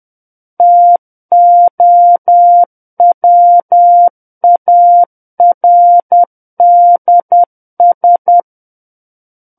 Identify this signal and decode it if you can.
Morse code, heard as a keyed tone, about 10 wpm, 700 Hz. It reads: TOWARDS
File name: Book One / 01-Chapter 1 - 10 WPM